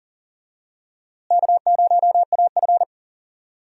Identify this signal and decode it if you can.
Morse code, heard as a keyed tone, about 40 wpm, 700 Hz. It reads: K0AF
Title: USA Call Signs - 4 Characters - Repeated 2x - 40wpm